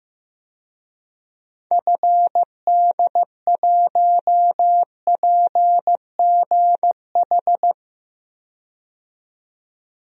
Morse code, heard as a keyed tone, about 15 wpm, 700 Hz. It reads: FD1PGH
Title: DXCC-100 Call Signs - 6 Characters - 15wpm